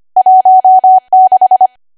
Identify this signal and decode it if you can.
Morse code, heard as a keyed tone, about 25 wpm, 750 Hz. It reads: 16
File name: Archive / 25 WPM / 25WPM